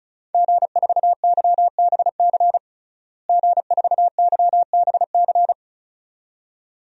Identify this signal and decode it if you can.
Morse code, heard as a keyed tone, about 34 wpm, 700 Hz. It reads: G4YBC G4YBC
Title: DXCC-100 Call Signs - 5 Characters - Repeated 2x - 35wpm